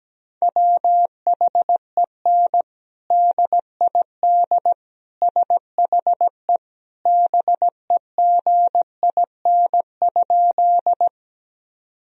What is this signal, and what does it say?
Morse code, heard as a keyed tone, about 17 wpm, 700 Hz. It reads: WHEN DID SHE BEGIN?
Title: Sentences from Top 1000 Words - Review Part 1 - 17wpm